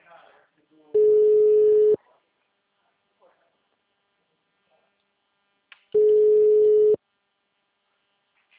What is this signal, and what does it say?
Ringback tone (Europe, 425 Hz), with speech.
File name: out- -2010- - - .9